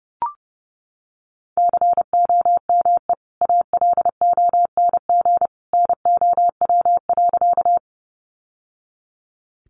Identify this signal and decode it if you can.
Morse code, heard as a keyed tone, about 30 wpm, 700 Hz. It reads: COME ALONG NOW.
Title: Sentences from Top 700 Words 30wpm